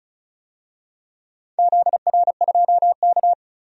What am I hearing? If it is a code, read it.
Morse code, heard as a keyed tone, about 34 wpm, 700 Hz. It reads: ZR2K
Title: DXCC-100 Call Signs - 4 Characters - 35wpm